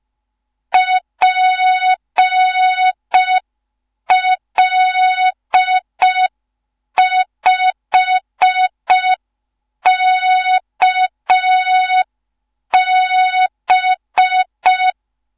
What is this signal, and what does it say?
Morse code, heard as a keyed tone, about 5 wpm, 3700 Hz. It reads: PL5KB
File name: Morze-HA2MN / hang / 15 GYAKORLAT